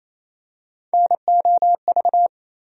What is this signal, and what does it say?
Morse code, heard as a keyed tone, about 28 wpm, 700 Hz. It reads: NOV
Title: USA Call Sign Postfixes - 3 Letters 28wpm